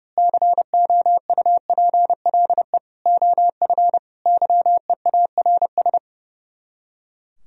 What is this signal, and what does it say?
Morse code, heard as a keyed tone, about 28 wpm, 700 Hz. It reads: COUPLE OF YEARS
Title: Speed-Racing - Sets of 5 Words 30wpm